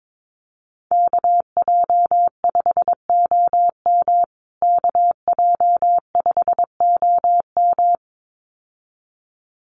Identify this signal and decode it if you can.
Morse code, heard as a keyed tone, about 22 wpm, 700 Hz. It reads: KJ5OM KJ5OM
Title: USA Call Signs Repeated 2x 22wpm